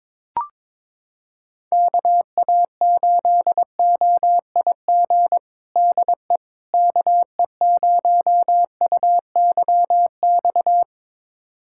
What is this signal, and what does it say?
Morse code, heard as a keyed tone, about 22 wpm, 700 Hz. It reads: KA8OIG DE KE0UYX
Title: USA Callsign DE Callsign - 6 Characters - 22wpm